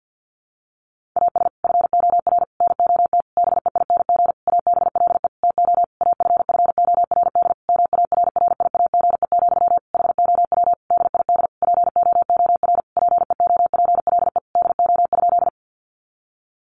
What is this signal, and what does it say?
Morse code, heard as a keyed tone, about 50 wpm, 700 Hz. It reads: AS FOR NOT BEING ABLE TO AFFORD MARRIAGE, HOW DID POOR PEOPLE DO?
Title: Sentences from Top 1100 Words 50wpm